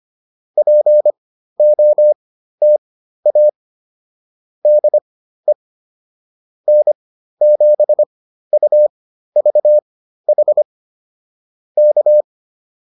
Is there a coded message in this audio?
Yes: Morse code from 0.6 to 12.2 s, heard as a keyed tone, about 25 wpm, spaced as for 13 wpm, 600 Hz. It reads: POTA DE N7UVH K